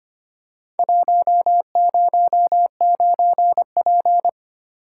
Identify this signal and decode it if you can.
Morse code, heard as a keyed tone, about 25 wpm, 700 Hz. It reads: 109P
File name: practice / 25wpm / Four Letter-Number Combinations - 25wpm